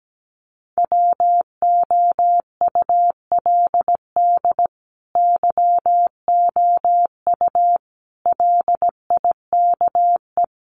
Morse code, heard as a keyed tone, about 17 wpm, 700 Hz. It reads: WOULD YOU LIKE
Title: Sets of 4 Words 17wpm